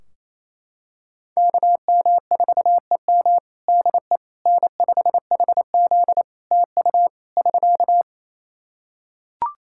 Morse code, heard as a keyed tone, about 28 wpm, 700 Hz. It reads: KM4EM DE N5HZ TU <SK>
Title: Practice QSO - Exchange 4 - Ending - 28wpm